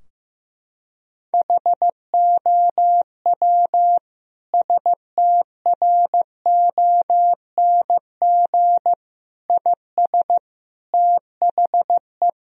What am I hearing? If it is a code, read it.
Morse code, heard as a keyed tone, about 15 wpm, 700 Hz. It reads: HOW STRONG IS THE
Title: Sentences from Top 1000 Words - Review Part 4 - 15wpm